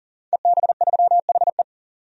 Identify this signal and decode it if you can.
Morse code, heard as a keyed tone, about 40 wpm, 700 Hz. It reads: EB3HE
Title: DXCC-100 Call Signs - 5 Characters - 40wpm